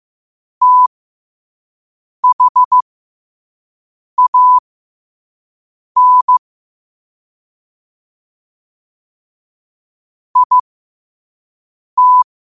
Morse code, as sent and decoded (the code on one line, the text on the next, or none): - .... .- -. / .. -
THAN IT